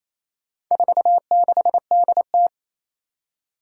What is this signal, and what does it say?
Morse code, heard as a keyed tone, about 28 wpm, 700 Hz. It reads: 46DT
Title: Four Challenging Letter-Number Combinations 28wpm